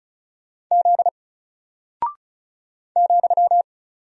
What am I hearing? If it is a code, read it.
Morse code, heard as a keyed tone, about 36 wpm, 700 Hz. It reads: Z ,
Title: Single Character (Letter:Number:Punctuation:Prosign) Alt 35wpm